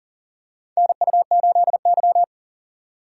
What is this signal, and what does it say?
Morse code, heard as a keyed tone, about 40 wpm, 700 Hz. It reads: NU8Y
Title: USA Call Signs - 4 Characters - 3x Character Spacing - 40wpm